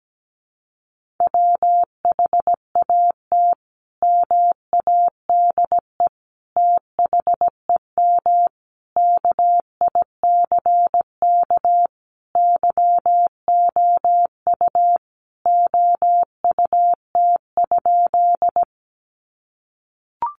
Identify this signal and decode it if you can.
Morse code, heard as a keyed tone, about 17 wpm, 700 Hz. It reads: WHAT MADE THEM KICK YOU OUT?
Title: Sentences from Top 1300 Words 17wpm